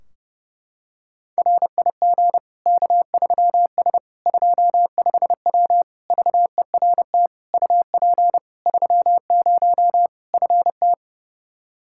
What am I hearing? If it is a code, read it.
Morse code, heard as a keyed tone, about 30 wpm, 700 Hz. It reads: RIG K3S 25W VERT UP 30 FT
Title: Practice QSO - Exchange 2 - Info - 30wpm